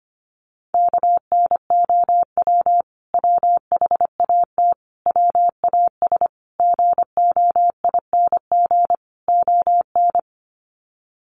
Morse code, heard as a keyed tone, about 25 wpm, 700 Hz. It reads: KNOW WHAT WAS GOING ON
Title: Sets of 5 Words Encore2 25wpm